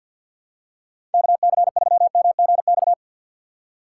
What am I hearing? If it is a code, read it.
Morse code, heard as a keyed tone, about 50 wpm, 700 Hz. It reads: KC3MGX
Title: USA Call Signs - 6 Characters - 6x Character Spacing - 50wpm